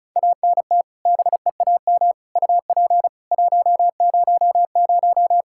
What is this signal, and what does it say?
Morse code, heard as a keyed tone, about 34 wpm, 700 Hz. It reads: ANT BEAM UP 100
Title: K1USN SST QSO Practice 35wpm